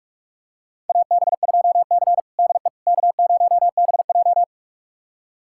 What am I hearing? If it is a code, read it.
Morse code, heard as a keyed tone, about 44 wpm, 700 Hz. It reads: AB2C DE K0BJ
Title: USA Callsign DE Callsign - 4 Characters - 45wpm